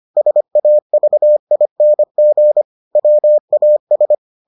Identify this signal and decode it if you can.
Morse code, heard as a keyed tone, about 25 wpm, 600 Hz. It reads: SAVING WAS